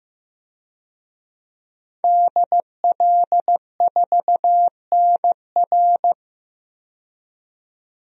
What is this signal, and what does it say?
Morse code, heard as a keyed tone, about 15 wpm, 700 Hz. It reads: DL4NR
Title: DXCC-100 Call Signs - 5 Characters - 15wpm